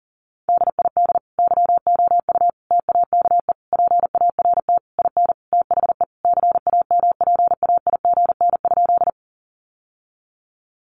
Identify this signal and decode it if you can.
Morse code, heard as a keyed tone, about 40 wpm, 700 Hz. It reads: DID YOU TAKE PART IN THE CAMPAIGN?
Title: Sentences from Top 1200 Words 40wpm